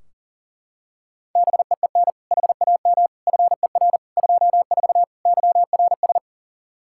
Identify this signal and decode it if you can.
Morse code, heard as a keyed tone, about 40 wpm, 700 Hz. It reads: BEEN HAM FER 24 YRS
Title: Practice QSO - Exchange 3 - More Info - 40wpm